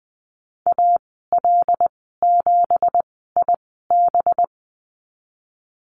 Morse code, heard as a keyed tone, about 20 wpm, spaced as for 15 wpm, 700 Hz. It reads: AL7IB
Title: 200 Lesson 32g - Letter k - Call Signs 20wpm